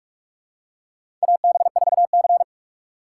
Morse code, heard as a keyed tone, about 45 wpm, 700 Hz. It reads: AB4C
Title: USA Call Signs - 4x Character Spacing - 45wpm